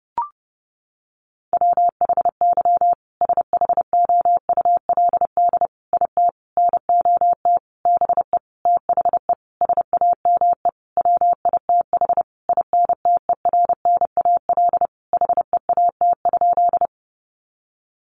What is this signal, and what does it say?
Morse code, heard as a keyed tone, about 30 wpm, 700 Hz. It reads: WHY SHOULD IT NOT BE THE SAME WITH INTERNAL HEAT?
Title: Sentences from Top 1400 Words 30wpm